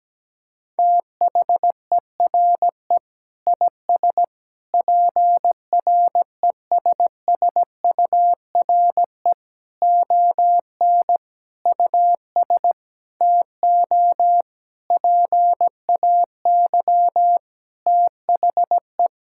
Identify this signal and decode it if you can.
Morse code, heard as a keyed tone, about 17 wpm, 700 Hz. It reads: THERE IS PRESSURE ON US TO PAY THE
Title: Sentences from Top 1000 Words - Review Part 6 - 17wpm